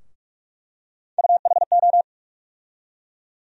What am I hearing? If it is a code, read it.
Morse code, heard as a keyed tone, about 47 wpm, 700 Hz. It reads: UHO
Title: USA Call Sign Postfixes - 3 Letters 45wpm